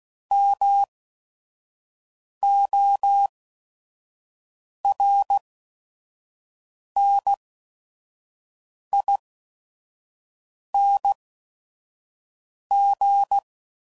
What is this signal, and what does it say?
Morse code, heard as a keyed tone, about 16 wpm, spaced as for 5 wpm, 800 Hz. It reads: MORNING
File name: CW Code Groups @5WPM / Weather Sets / Weather 4 28